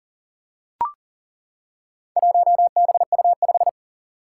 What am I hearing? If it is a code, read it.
Morse code, heard as a keyed tone, about 42 wpm, 700 Hz. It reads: 1BU5
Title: Four Challenging Letter-Number Combinations 40wpm